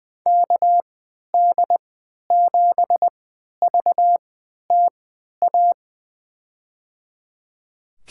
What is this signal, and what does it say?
Morse code, heard as a keyed tone, about 20 wpm, spaced as for 11 wpm, 700 Hz. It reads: KD7VTA
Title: USA Call Signs - 6 Characters - 3x Character Spacing - 20wpm